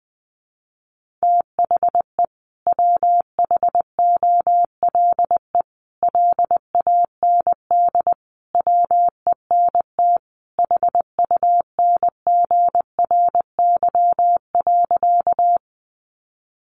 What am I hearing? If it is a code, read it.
Morse code, heard as a keyed tone, about 19 wpm, 700 Hz. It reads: THE WHOLE LAND WENT HUNGRY.
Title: Sentences from Top 1400 Words 20wpm